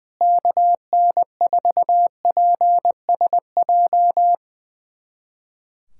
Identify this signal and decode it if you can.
Morse code, heard as a keyed tone, about 20 wpm, 700 Hz. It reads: KN4PSJ
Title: USA Callsign DE Callsign 20wpm